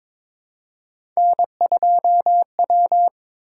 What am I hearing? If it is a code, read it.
Morse code, heard as a keyed tone, about 22 wpm, 700 Hz. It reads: N2W
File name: practice / 22wpm / Three Letter-Number Combinations - 22wpm